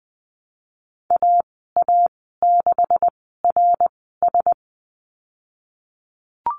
Morse code, heard as a keyed tone, about 20 wpm, spaced as for 15 wpm, 700 Hz. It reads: AA6RS
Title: 167 Lesson 27g - Number 7 - Call Signs 20wpm